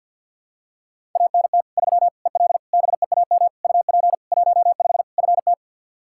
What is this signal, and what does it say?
Morse code, heard as a keyed tone, about 53 wpm, 700 Hz. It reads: ANT 3 EL BEAM UP 15 FT
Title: Practice QSO - Exchange 2 - Info - 50wpm